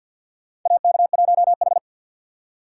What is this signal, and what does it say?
Morse code, heard as a keyed tone, about 46 wpm, 700 Hz. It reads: AK1H
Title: USA Call Signs - 4 Characters - 2x Character Spacing - 50wpm